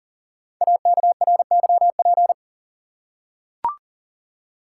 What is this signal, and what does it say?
Morse code, heard as a keyed tone, about 39 wpm, 690 Hz. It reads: AKRYP
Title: Code Groups - 5 Letters - 40wpm